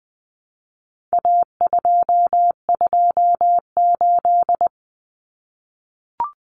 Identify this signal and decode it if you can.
Morse code, heard as a keyed tone, about 20 wpm, 700 Hz. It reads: A228
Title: Common Cut Numbers - 4-Digit 20wpm